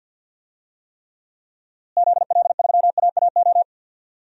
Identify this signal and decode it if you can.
Morse code, heard as a keyed tone, about 53 wpm, 700 Hz. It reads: ZL3AAO
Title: DXCC-100 Call Signs - 6 Characters - Repeated 2x - 50wpm